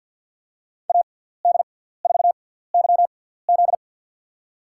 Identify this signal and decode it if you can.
Morse code, heard as a keyed tone, about 53 wpm, spaced as for 17 wpm, 700 Hz. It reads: AD4YZ
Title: USA Call Signs - 6x Character Spacing - 50wpm